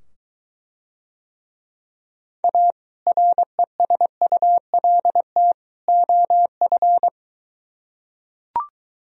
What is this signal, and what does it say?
Morse code, heard as a keyed tone, about 21 wpm, 700 Hz. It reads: A RESULT OF